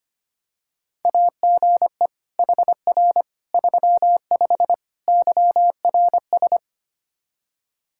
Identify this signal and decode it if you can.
Morse code, heard as a keyed tone, about 25 wpm, 700 Hz. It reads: AGE HR 35 YRS
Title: Practice QSO - Exchange 3 - More Info - 25wpm